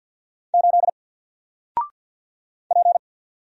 Morse code, heard as a keyed tone, about 50 wpm, 700 Hz. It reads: 8 P
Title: Single Letter-Number 50wpm